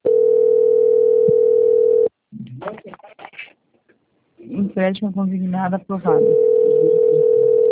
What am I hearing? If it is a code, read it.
Ringback tone (North America), 440 + 480 Hz, with speech.
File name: out- - - - - .9504